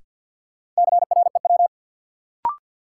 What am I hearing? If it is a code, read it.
Morse code, heard as a keyed tone, about 52 wpm, 700 Hz. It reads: CREW
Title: Sentences from Top 1500 Words - New Word Drill - 50wpm